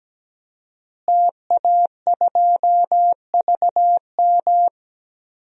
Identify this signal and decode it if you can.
Morse code, heard as a keyed tone, about 17 wpm, 700 Hz. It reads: TA2VM
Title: DXCC-100 Call Signs - 5 Characters - Repeated 3x - 17wpm